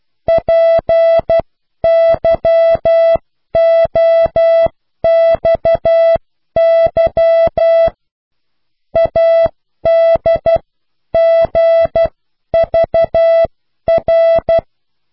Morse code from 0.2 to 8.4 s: PYOXY